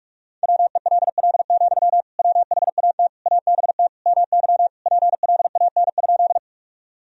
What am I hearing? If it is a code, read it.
Morse code, heard as a keyed tone, about 46 wpm, 700 Hz. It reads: WELL, WHAT ABT MY PLAN?
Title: Sentences from Top 500 Words and Abbreviations 45wpm